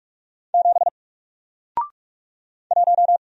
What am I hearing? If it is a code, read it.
Morse code, heard as a keyed tone, about 46 wpm, 700 Hz. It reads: 7 1
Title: Single Letter-Number Alt 45wpm